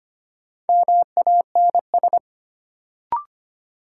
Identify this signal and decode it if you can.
Morse code, heard as a keyed tone, about 25 wpm, 700 Hz. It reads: MANS
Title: Sentences from Top 1000 Words - Review Part 6 - New Word Drill - 25wpm